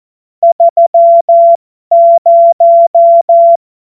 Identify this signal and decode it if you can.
Morse code, heard as a keyed tone, about 14 wpm, spaced as for 12 wpm, 680 Hz. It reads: 30